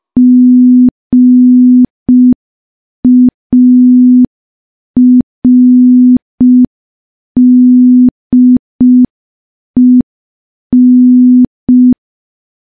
Morse code, heard as a keyed tone, about 5 wpm, 250 Hz. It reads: GARDEN
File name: morse / morse-training-05WPM